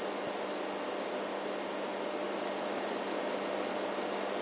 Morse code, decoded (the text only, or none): none